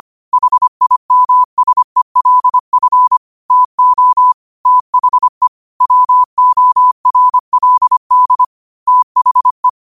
Morse code, heard as a keyed tone, about 25 wpm, 1000 Hz. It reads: HIMSELF TO THE WORLD THE
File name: qotdfiles / 25wpm